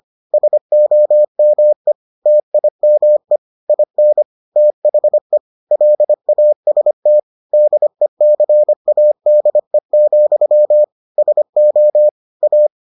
Morse code, heard as a keyed tone, about 25 wpm, 600 Hz. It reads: SOME TIME IN THE LAST DECADE, SO A